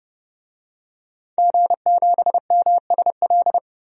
Morse code, heard as a keyed tone, about 32 wpm, 700 Hz. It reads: G7MSL
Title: DXCC-100 Call Signs - 5 Characters - 30wpm